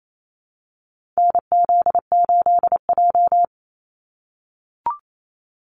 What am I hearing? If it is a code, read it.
Morse code, heard as a keyed tone, about 28 wpm, 700 Hz. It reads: NZ8J
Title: Call Signs - 28wpm